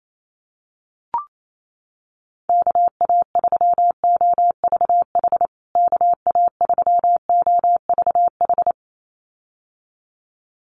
Morse code, heard as a keyed tone, about 28 wpm, 700 Hz. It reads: KA3OVH KA3OVH
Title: USA Call Signs Repeated 2x 28wpm